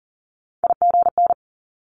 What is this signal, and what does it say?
Morse code, heard as a keyed tone, about 38 wpm, 700 Hz. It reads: IGN
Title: Multi-letter Phonemes Quick Fire 40wpm